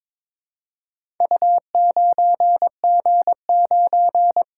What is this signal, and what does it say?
Morse code, heard as a keyed tone, about 22 wpm, 700 Hz. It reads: U9G9
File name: practice / 22wpm / Four Letter-Number Combinations - 22wpm